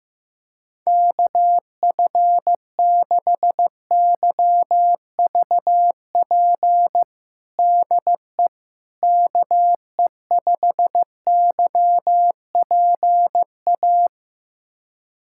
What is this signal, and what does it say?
Morse code, heard as a keyed tone, about 15 wpm, 700 Hz. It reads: KF6YVP DE KE5YPA
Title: USA Callsign DE Callsign 15wpm